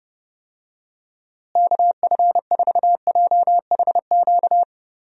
Morse code, heard as a keyed tone, about 30 wpm, 700 Hz. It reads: KF4JHQ
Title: USA Call Signs - 6 Characters - Repeated 3x - 30wpm